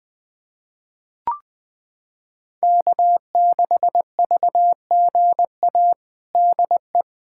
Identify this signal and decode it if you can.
Morse code, heard as a keyed tone, about 21 wpm, 700 Hz. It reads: K6VGA DE
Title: USA Callsign DE Callsign 20wpm